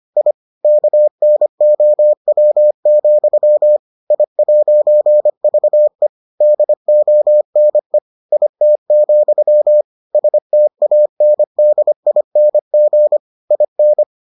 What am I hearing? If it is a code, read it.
Morse code, heard as a keyed tone, about 25 wpm, 600 Hz. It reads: I KNOW, I'VE DONE IT, STANDING IN